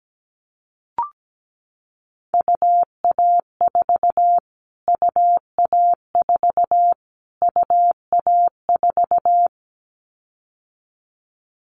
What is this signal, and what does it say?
Morse code, heard as a keyed tone, about 17 wpm, 700 Hz. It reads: UA4 UA4 UA4
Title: Callsign Prefixes for Top 25 DXCC - Repeated 3x - 17wpm